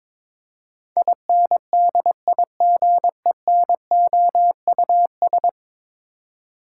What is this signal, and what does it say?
Morse code, heard as a keyed tone, about 22 wpm, 700 Hz. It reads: INDIGENOUS
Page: morse-code-ninja-practice-22wpm